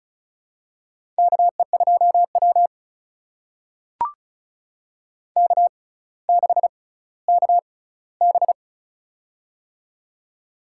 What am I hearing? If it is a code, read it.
Morse code, heard as a keyed tone, about 34 wpm, spaced as for 18 wpm, 700 Hz. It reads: KE2W K6KB
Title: USA Call Signs - 4 Characters - 6x Character Spacing - 35wpm